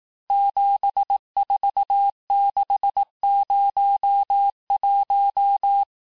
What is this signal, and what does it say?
Morse code, heard as a keyed tone, about 18 wpm, 800 Hz. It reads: 74601
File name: cours-du-lundi files / cours-1852-texte-chiffre-vitesse-17MPM